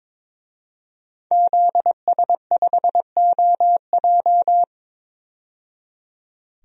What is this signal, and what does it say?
Morse code, heard as a keyed tone, about 22 wpm, 700 Hz. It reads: ZS5OJ
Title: DXCC-100 Call Signs - 5 Characters - 22wpm